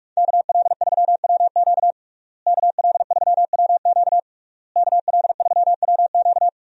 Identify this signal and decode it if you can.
Morse code, heard as a keyed tone, about 41 wpm, 700 Hz. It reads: KL3WQ KL3WQ KL3WQ
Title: USA Call Signs - 5 Characters - Repeated 3x - 45wpm